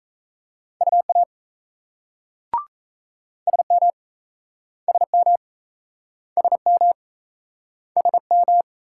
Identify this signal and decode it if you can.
Morse code, heard as a keyed tone, about 39 wpm, 700 Hz. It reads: UA SM SM SM SM